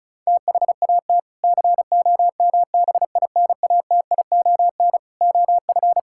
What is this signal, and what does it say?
Morse code, heard as a keyed tone, about 36 wpm, 700 Hz. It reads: THAT COMBINATION OF
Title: Sentences from Top 1000 Words - Review Part 6 - 35wpm